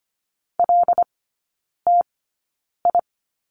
Morse code, heard as a keyed tone, about 25 wpm, spaced as for 9 wpm, 690 Hz. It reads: LTI
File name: Lesson 14e - Number 2 - Three Characters 25wpm